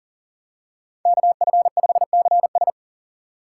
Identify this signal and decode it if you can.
Morse code, heard as a keyed tone, about 40 wpm, 700 Hz. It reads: KF5CS
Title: USA Call Signs - 5 Characters - 4x Character Spacing - 40wpm